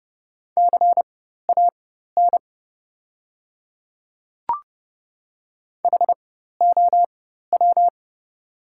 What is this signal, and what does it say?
Morse code, heard as a keyed tone, about 30 wpm, spaced as for 14 wpm, 700 Hz. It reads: CAN HOW
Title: 4x Character Spacing - Top 100 Words - 30wpm